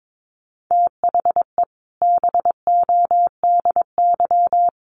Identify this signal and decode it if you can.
Morse code, heard as a keyed tone, about 22 wpm, 700 Hz. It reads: THE BODY
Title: A Sound Of Thunder 22wpm